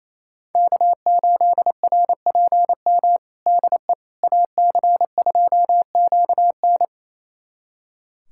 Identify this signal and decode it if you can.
Morse code, heard as a keyed tone, about 28 wpm, 700 Hz. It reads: K8RPM DE AC2QN
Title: USA Callsign DE Callsign - 5 Characters - 28wpm